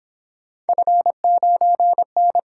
Morse code, heard as a keyed tone, about 26 wpm, 700 Hz. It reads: F9N
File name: Speed-Racing - Three Letter-Number Combinations - 22wpm